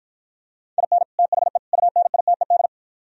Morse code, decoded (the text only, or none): IN THE UNITED